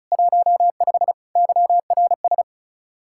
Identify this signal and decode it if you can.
Morse code, heard as a keyed tone, about 45 wpm, 700 Hz. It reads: 15 YRS